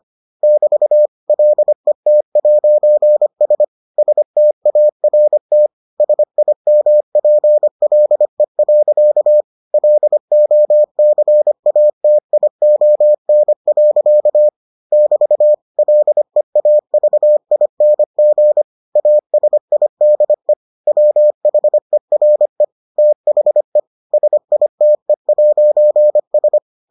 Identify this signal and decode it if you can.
Morse code, heard as a keyed tone, about 25 wpm, 600 Hz. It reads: <BT> LET'S START SIMPLE. LOCATION. <BT> LEAVING ASIDE WHERE THE SITE'S